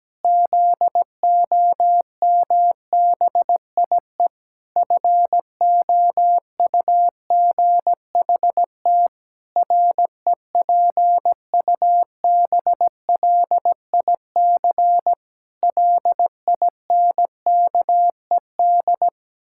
Morse code, heard as a keyed tone, about 17 wpm, 700 Hz. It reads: ZOMBIE FOUGHT REPUBLIC LINKED
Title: Pangram Challenge v2 - 17wpm